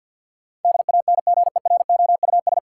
Morse code, heard as a keyed tone, about 52 wpm, 700 Hz. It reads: DANGEROUS
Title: Sentences from Top 1200 Words - New Word Drill Alt - 50wpm